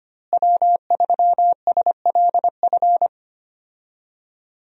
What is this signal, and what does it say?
Morse code, heard as a keyed tone, about 25 wpm, 700 Hz. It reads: W3SLF
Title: USA Call Signs - 5 Characters - Repeated 3x - 25wpm